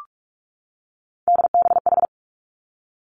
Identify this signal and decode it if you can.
Morse code, heard as a keyed tone, about 46 wpm, 700 Hz. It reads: DBH
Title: Three Challenging Letter-Number Combinations 45wpm